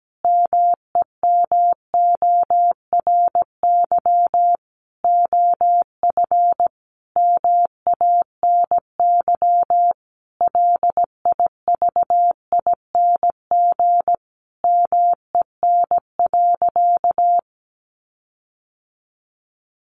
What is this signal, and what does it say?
Morse code, heard as a keyed tone, about 17 wpm, 700 Hz. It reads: MEMORY OF MANY LIVING MEN.
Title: Sentences from Top 1300 Words 17wpm